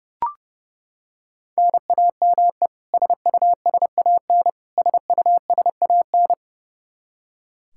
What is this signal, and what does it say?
Morse code, heard as a keyed tone, about 30 wpm, 700 Hz. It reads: NAME SUSAN SUSAN
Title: Practice QSO - Exchange 1 - Beginning - 30wpm